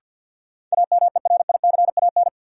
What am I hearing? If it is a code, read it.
Morse code, heard as a keyed tone, about 46 wpm, 700 Hz. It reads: AMERICAN
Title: Sets of 2 Words - New Word Drill - 50wpm